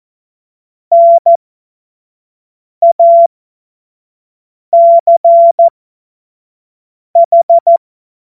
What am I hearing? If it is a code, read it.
Morse code, heard as a keyed tone, about 14 wpm, spaced as for 5 wpm, 680 Hz. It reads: NACH
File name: Morsekurs / Witze / Witz 30 Januar 2024 25 F70